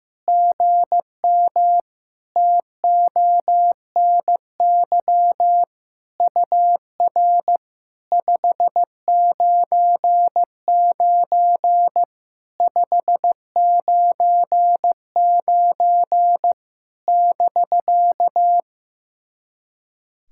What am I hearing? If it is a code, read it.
Morse code, heard as a keyed tone, about 15 wpm, 700 Hz. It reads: GM TONY UR 599 599 <BK>
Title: SOTA QSO Practice 15wpm